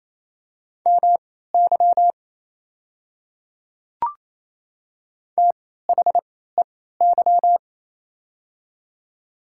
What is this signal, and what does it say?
Morse code, heard as a keyed tone, about 28 wpm, spaced as for 16 wpm, 700 Hz. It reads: MY THEY
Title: 3x Character Spacing - Top 100 Words - 28wpm